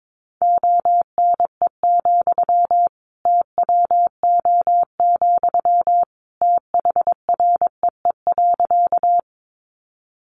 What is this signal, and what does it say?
Morse code, heard as a keyed tone, about 22 wpm, 700 Hz. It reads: ONE, TWO, THREE.
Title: Sentences from Top 200 Words 22wpm